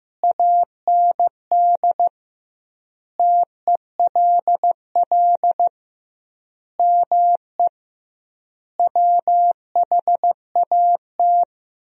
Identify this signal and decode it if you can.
Morse code, heard as a keyed tone, about 15 wpm, 700 Hz. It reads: AND TELL ME WHAT
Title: 2x Word-Spacing - Sentences from Top 200 Words - 15wpm